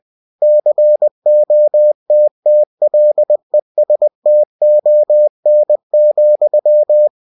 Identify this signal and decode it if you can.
Morse code, heard as a keyed tone, about 20 wpm, 600 Hz. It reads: COTTLESTON,
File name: cw-books / winnie-the-pooh / winnie-20wpm-page-0046